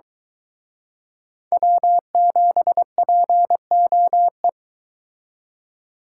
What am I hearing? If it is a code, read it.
Morse code, heard as a keyed tone, about 23 wpm, 700 Hz. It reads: W7POE